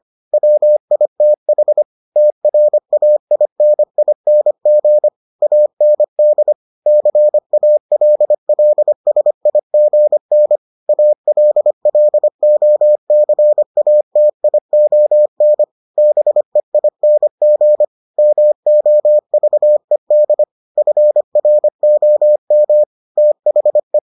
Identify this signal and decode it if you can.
Morse code, heard as a keyed tone, about 25 wpm, 600 Hz. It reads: WITH TRAINING AND CALLSIGN ALLOCATION BEING MOVED FROM THE